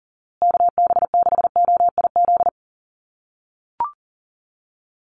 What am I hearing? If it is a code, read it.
Morse code, heard as a keyed tone, about 40 wpm, 700 Hz. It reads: KB6OIZ
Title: USA Call Signs Repeated 2x 40wpm